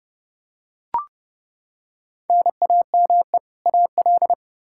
Morse code, heard as a keyed tone, about 29 wpm, 700 Hz. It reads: NAME AL
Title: Practice QSO - Exchange 1 - Beginning - 30wpm